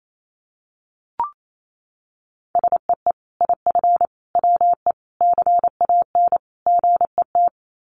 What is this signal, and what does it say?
Morse code, heard as a keyed tone, about 28 wpm, 700 Hz. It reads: SEE IF WE CAN GET
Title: Sets of 5 Words Encore2 28wpm